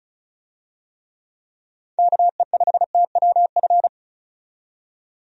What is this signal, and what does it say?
Morse code, heard as a keyed tone, about 34 wpm, 700 Hz. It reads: KE5TWF
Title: USA Call Signs - 6 Characters - 35wpm